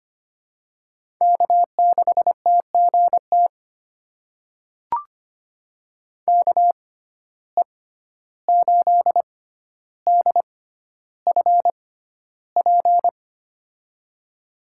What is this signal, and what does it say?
Morse code, heard as a keyed tone, about 25 wpm, spaced as for 12 wpm, 700 Hz. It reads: K6TGT KE8DFP
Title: USA Call Signs - 6x Character Spacing - 25wpm